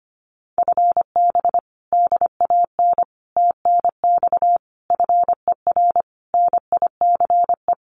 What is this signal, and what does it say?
Morse code, heard as a keyed tone, about 25 wpm, 700 Hz. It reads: FB DAN TNX FER NICE
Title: Common QSO Phrases Encore 25wpm